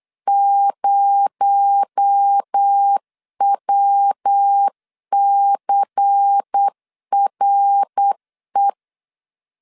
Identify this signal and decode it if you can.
Morse code, heard as a keyed tone, about 8 wpm, 800 Hz. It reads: 0WCRE
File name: CW / DET / 1 6 1500C S B 8wpm 5142